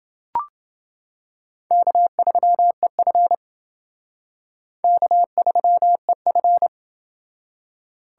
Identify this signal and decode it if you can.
Morse code, heard as a keyed tone, about 29 wpm, 700 Hz. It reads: K3EF K3EF